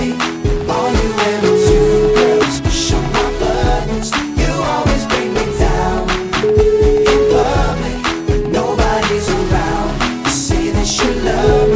Ringback tone (Europe, 425 Hz), with noise.